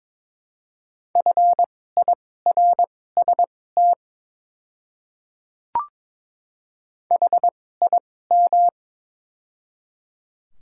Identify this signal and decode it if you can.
Morse code, heard as a keyed tone, about 22 wpm, spaced as for 16 wpm, 700 Hz. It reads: FIRST HIM
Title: 2x Character Spacing - Top 100 Words - 22wpm